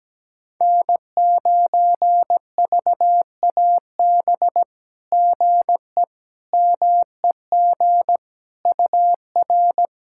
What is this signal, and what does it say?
Morse code, heard as a keyed tone, about 17 wpm, 700 Hz. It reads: N9VAB GE MEG UR
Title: SOTA QSO Practice 17wpm